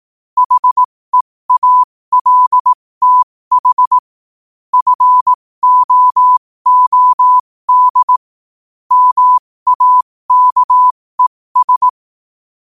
Morse code, heard as a keyed tone, about 18 wpm, spaced as for 15 wpm, 1000 Hz. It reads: HEALTH FOOD MAKES
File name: qotdfiles / 15wpm